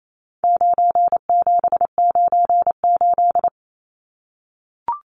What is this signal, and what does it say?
Morse code, heard as a keyed tone, about 28 wpm, 700 Hz. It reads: 9798
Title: Numbers - 4-Digit 28wpm